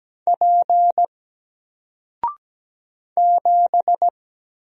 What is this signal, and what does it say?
Morse code, heard as a keyed tone, about 17 wpm, 700 Hz. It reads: P 7
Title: Single Letter-Number 17wpm